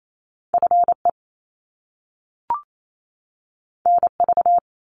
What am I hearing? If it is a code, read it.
Morse code, heard as a keyed tone, about 28 wpm, 700 Hz. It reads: FE NV
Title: Top Two Letter Combinations in Words 28wpm